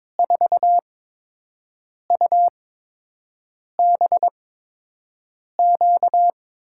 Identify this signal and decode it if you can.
Morse code, heard as a keyed tone, about 22 wpm, spaced as for 6 wpm, 700 Hz. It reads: 4UBQ
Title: USA Call Signs - 6 Characters - 8x Character Spacing - 22wpm